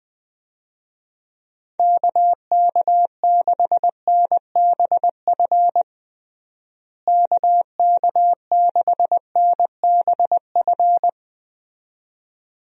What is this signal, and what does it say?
Morse code, heard as a keyed tone, about 20 wpm, 700 Hz. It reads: KK6NBF KK6NBF
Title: USA Call Signs - 6 Characters - Repeated 2x - 20wpm